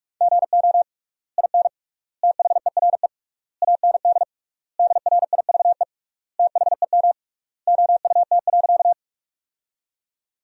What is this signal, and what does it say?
Morse code, heard as a keyed tone, about 41 wpm, 700 Hz. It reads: GO IN THERE AND DRIVE THEM OUT.
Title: 3x Word-Spacing - Sentences from Top 500 Words - 45wpm